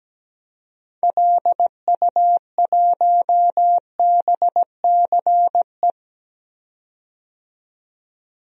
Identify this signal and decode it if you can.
Morse code, heard as a keyed tone, about 17 wpm, 700 Hz. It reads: LU1BCE
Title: DXCC-100 Call Signs - 6 Characters - 17wpm